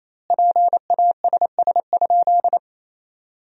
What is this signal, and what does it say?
Morse code, heard as a keyed tone, about 28 wpm, 700 Hz. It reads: PASS?
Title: 2x Word-Spacing - Sentences from Top 400 Words - 28wpm